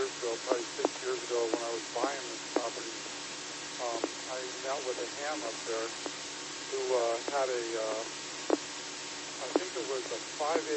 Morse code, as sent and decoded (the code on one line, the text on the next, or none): none